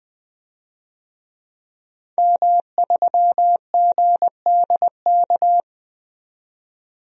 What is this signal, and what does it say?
Morse code, heard as a keyed tone, about 20 wpm, 700 Hz. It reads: M3GDK
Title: DXCC-100 Call Signs - 5 Characters - 20wpm